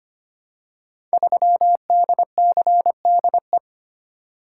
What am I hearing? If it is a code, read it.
Morse code, heard as a keyed tone, about 25 wpm, 700 Hz. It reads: 3DCDE
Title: Code Groups - 5 Letter:Numbers - 25wpm